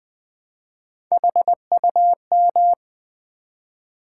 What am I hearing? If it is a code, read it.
Morse code, heard as a keyed tone, about 20 wpm, 700 Hz. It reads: HUM
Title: Build A Word - 1 Letter At A Time - 5-Letter Words - 20wpm